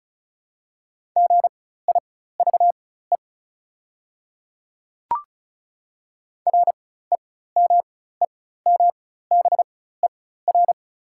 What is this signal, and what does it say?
Morse code, heard as a keyed tone, about 36 wpm, spaced as for 16 wpm, 690 Hz. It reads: GIVE REMEMBER
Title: 4x Character Spacing - Top 500 Words - 35wpm